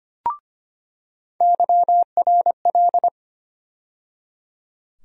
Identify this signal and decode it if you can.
Morse code, heard as a keyed tone, about 25 wpm, 700 Hz. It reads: YRL
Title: USA Call Sign Postfixes - 3 Letters 25wpm